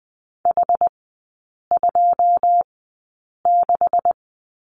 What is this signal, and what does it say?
Morse code, heard as a keyed tone, about 20 wpm, spaced as for 8 wpm, 700 Hz. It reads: H26
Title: 198 Lesson 32e - Letter k - Three Characters 20wpm